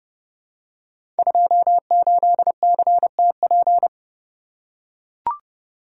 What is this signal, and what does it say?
Morse code, heard as a keyed tone, about 30 wpm, 700 Hz. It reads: 28CTP
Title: Code Groups - 5 Letter:Numbers - 30wpm